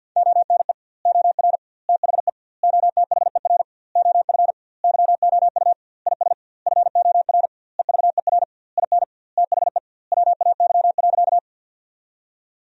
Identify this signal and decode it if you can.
Morse code, heard as a keyed tone, about 50 wpm, 700 Hz. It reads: ONE OR THE OTHER OF YOU IS FOR EVER IN THE WAY.
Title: 2x Word-Spacing - Sentences from Top 400 Words - 50wpm